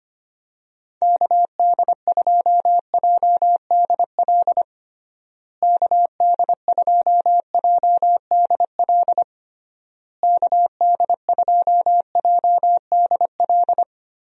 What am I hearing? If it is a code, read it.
Morse code, heard as a keyed tone, about 25 wpm, 700 Hz. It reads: KD2JDL KD2JDL KD2JDL
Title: USA Call Signs - 6 Characters - Repeated 3x - 25wpm